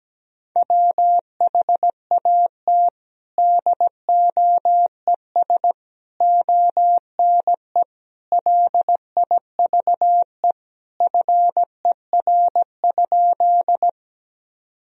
Morse code, heard as a keyed tone, about 17 wpm, 700 Hz. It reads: WHAT DOES ONE LIVE FER?
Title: Sentences from Top 200 Words and Abbreviations 17wpm